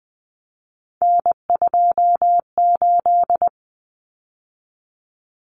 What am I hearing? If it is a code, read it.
Morse code, heard as a keyed tone, about 20 wpm, 700 Hz. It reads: N28
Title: Common Cut Numbers - 3-Digit 20wpm